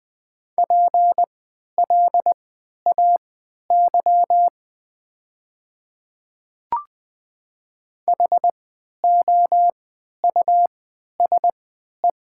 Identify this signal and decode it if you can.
Morse code, heard as a keyed tone, about 20 wpm, spaced as for 11 wpm, 700 Hz. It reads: PLAY HOUSE
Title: 3x Character Spacing - Top 200 Words - 20wpm